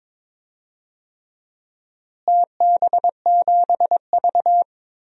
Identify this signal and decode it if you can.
Morse code, heard as a keyed tone, about 22 wpm, 700 Hz. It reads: TB7V
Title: DXCC-100 Call Signs - 4 Characters - Repeated 3x - 22wpm